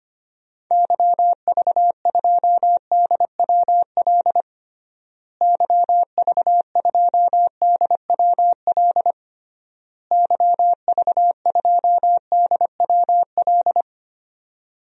Morse code, heard as a keyed tone, about 25 wpm, 700 Hz. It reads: YV2DWL YV2DWL YV2DWL
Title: DXCC-100 Call Signs - 6 Characters - Repeated 3x - 25wpm